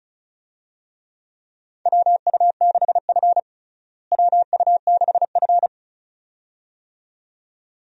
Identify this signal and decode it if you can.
Morse code, heard as a keyed tone, about 36 wpm, 700 Hz. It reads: WU6F WU6F
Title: USA Call Signs - 4 Characters - Repeated 2x - 35wpm